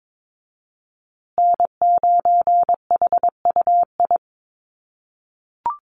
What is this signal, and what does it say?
Morse code, heard as a keyed tone, about 22 wpm, 700 Hz. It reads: N9HUI
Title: USA Call Signs Repeated 3x 22wpm